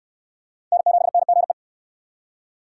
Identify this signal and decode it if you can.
Morse code, heard as a keyed tone, about 66 wpm, 700 Hz. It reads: N7NGE